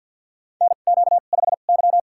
Morse code, heard as a keyed tone, about 53 wpm, spaced as for 37 wpm, 700 Hz. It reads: NQ5Y
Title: USA Call Signs - 4 Characters - 2x Character Spacing - 50wpm